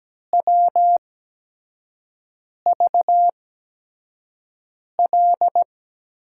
Morse code, heard as a keyed tone, about 17 wpm, spaced as for 5 wpm, 700 Hz. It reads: WVL
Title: USA Call Signs - 6 Characters - 8x Character Spacing - 17wpm